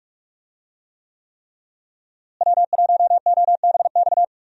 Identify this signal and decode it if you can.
Morse code, heard as a keyed tone, about 44 wpm, 700 Hz. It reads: W1OBX
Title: USA Call Signs - 5 Characters - Repeated 3x - 45wpm